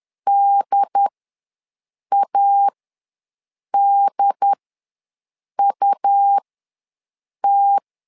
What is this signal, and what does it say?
Morse code, heard as a keyed tone, about 10 wpm, spaced as for 6 wpm, 800 Hz. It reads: DADUT
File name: LEZIONI / 2 UHDN / 2 UHD ws 10wpm 2058